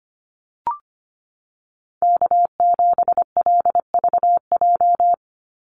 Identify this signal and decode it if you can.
Morse code, heard as a keyed tone, about 25 wpm, 700 Hz. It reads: K7LVJ
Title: Call Signs Repeated 3x - 25wpm